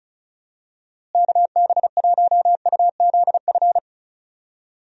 Morse code, heard as a keyed tone, about 34 wpm, 700 Hz. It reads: KB1UZF
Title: USA Call Signs - 6 Characters - Repeated 2x - 35wpm